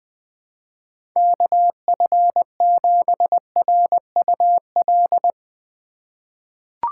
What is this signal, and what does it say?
Morse code, heard as a keyed tone, about 20 wpm, 700 Hz. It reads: KF7RUL
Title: USA Call Signs - 6 Characters - 2x Character Spacing - 20wpm